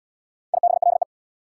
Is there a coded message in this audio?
Yes: Morse code from 0.5 to 1.0 s, heard as a keyed tone, about 65 wpm, 690 Hz. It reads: IBLE